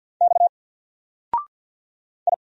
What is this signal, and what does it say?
Morse code, heard as a keyed tone, about 48 wpm, 700 Hz. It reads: X I
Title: Single Character (Letter:Number:Punctuation:Prosign) 50wpm